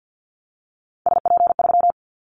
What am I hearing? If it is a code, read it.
Morse code, heard as a keyed tone, about 50 wpm, 700 Hz. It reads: SP3
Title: Callsign Prefixes for Top 25 DXCC - 50wpm